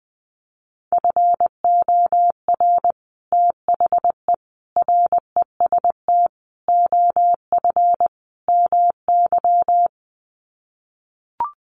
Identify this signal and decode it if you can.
Morse code, heard as a keyed tone, about 20 wpm, 700 Hz. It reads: FOR THE REST OF MY
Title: Sets of 5 Words 20wpm